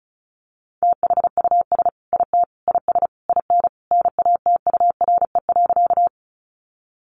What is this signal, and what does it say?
Morse code, heard as a keyed tone, about 34 wpm, 700 Hz. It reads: THUS IT IS IN NATURE.
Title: Sentences from Top 800 Words 35wpm